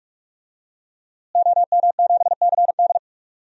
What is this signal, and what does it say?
Morse code, heard as a keyed tone, about 46 wpm, 700 Hz. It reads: OM7CD
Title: DXCC-100 Call Signs - 5 Characters - Repeated 3x - 45wpm